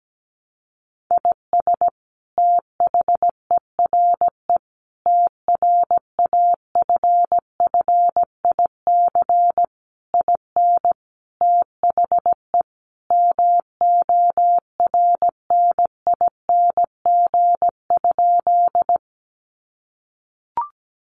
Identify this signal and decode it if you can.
Morse code, heard as a keyed tone, about 17 wpm, 700 Hz. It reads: IS THERE TRAFFIC IN THE MORNING?
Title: Sentences from Top 1300 Words 17wpm